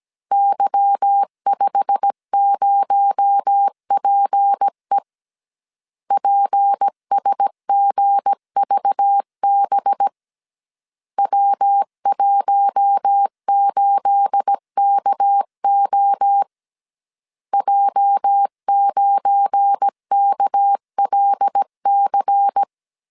Morse code, heard as a keyed tone, about 17 wpm, 800 Hz. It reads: Y50PE PSGVB W18KO J9KLC
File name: CW / DET / 1 9 1500C S C 16wpm 2909